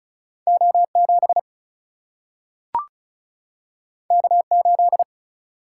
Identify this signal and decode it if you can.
Morse code, heard as a keyed tone, about 36 wpm, 700 Hz. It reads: O7 K8
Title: Single-Letter Single-Number 35wpm